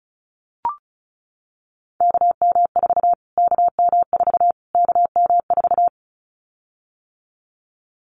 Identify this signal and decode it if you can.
Morse code, heard as a keyed tone, about 34 wpm, 700 Hz. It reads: KM4 KM4 KM4
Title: Callsign Prefixes for Top 25 DXCC - Repeated 3x - 35wpm